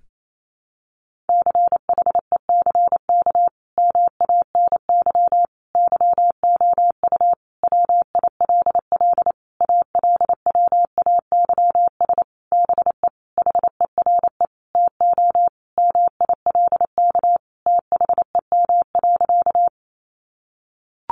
Morse code, heard as a keyed tone, about 28 wpm, 700 Hz. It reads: CHECK MANY YOU WILL ALWAYS BE HERE TO MILK THEM.
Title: Sentences from Top 1500 Words 28wpm